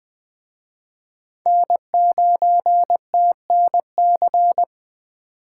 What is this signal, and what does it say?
Morse code, heard as a keyed tone, about 20 wpm, 700 Hz. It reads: N9TNC
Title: Speed Racing - USA 1x3 Call Signs 17wpm